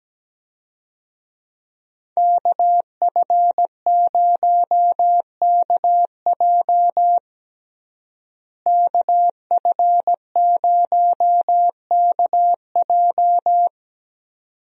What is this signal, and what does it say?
Morse code, heard as a keyed tone, about 17 wpm, 700 Hz. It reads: KF0KJ KF0KJ
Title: USA Call Signs - 5 Characters - Repeated 2x - 17wpm